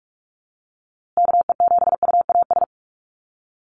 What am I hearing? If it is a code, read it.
Morse code, heard as a keyed tone, about 44 wpm, 700 Hz. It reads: KE7UAS
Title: USA Call Signs Repeated 3x 45wpm